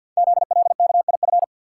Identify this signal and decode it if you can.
Morse code, heard as a keyed tone, about 48 wpm, 700 Hz. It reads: ZLKIF
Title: Code Groups - 5 Letters - 50wpm